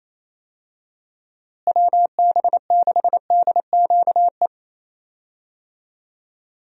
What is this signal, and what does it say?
Morse code, heard as a keyed tone, about 27 wpm, 700 Hz. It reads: WB6DQE
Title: USA Call Signs - 6 Characters - Repeated 2x - 28wpm